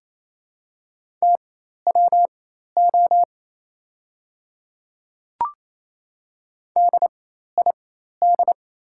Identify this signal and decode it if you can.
Morse code, heard as a keyed tone, about 28 wpm, spaced as for 13 wpm, 700 Hz. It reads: TWO DID
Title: 4x Character Spacing - Top 300 Words - 28wpm